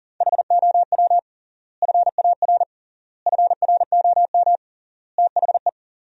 Morse code, heard as a keyed tone, about 40 wpm, 700 Hz. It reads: HOW FAR FROM THE
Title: 3x Word-Spacing - Sentences from Top 400 Words - 40wpm